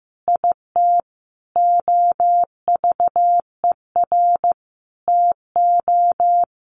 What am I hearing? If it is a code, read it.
Morse code, heard as a keyed tone, about 15 wpm, 700 Hz. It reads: IT OVER TO
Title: Sentences from Top 900 Words 15wpm